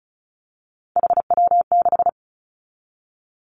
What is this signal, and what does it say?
Morse code, heard as a keyed tone, about 36 wpm, 700 Hz. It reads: HW6
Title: Three Challenging Letter-Number Combinations 35wpm